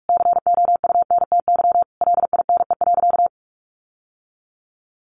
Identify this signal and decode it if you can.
Morse code, heard as a keyed tone, about 44 wpm, 700 Hz. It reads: COUNTY LINE.
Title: Sentences from Top 1200 Words 45wpm